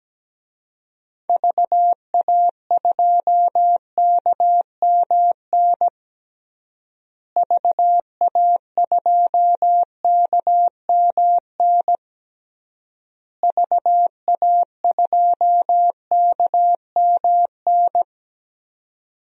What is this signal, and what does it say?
Morse code, heard as a keyed tone, about 17 wpm, 700 Hz. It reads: VA2KMN VA2KMN VA2KMN
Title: DXCC-100 Call Signs - 6 Characters - Repeated 3x - 17wpm